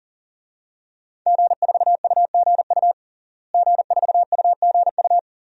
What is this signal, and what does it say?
Morse code, heard as a keyed tone, about 40 wpm, 700 Hz. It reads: G4UGU G4UGU
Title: DXCC-100 Call Signs - 5 Characters - Repeated 2x - 40wpm